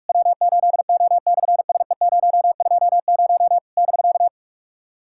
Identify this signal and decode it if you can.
Morse code, heard as a keyed tone, about 46 wpm, 700 Hz. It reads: W8O/SE020 <BK>
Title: SOTA QSO Practice 45wpm